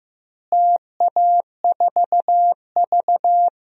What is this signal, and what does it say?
Morse code, heard as a keyed tone, about 15 wpm, 700 Hz. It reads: TA4V
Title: DXCC-100 Call Signs - 4 Characters - Repeated 3x - 15wpm